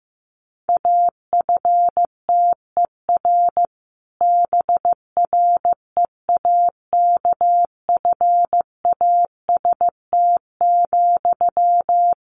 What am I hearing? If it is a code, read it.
Morse code, heard as a keyed tone, about 15 wpm, 700 Hz. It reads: AFTER BREAKFAST,
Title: Sentences from Top 1500 Words 15wpm